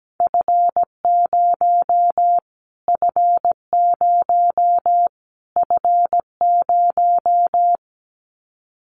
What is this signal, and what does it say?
Morse code, heard as a keyed tone, about 17 wpm, 700 Hz. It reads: F0 F0 F0
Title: Callsign Prefixes for Top 25 DXCC - Repeated 3x - 17wpm